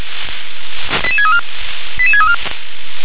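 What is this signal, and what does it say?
Five-tone selective call (ZVEI), heard as quick runs of five tones, tones of about 70 ms. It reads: 88634 88634
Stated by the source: Received on an SDR, ZVEI five-tone selective call, code 88742.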